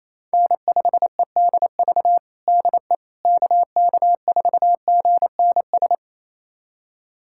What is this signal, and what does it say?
Morse code, heard as a keyed tone, about 28 wpm, 700 Hz. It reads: N5EDV DE KK4GNS
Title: USA Callsign DE Callsign 28wpm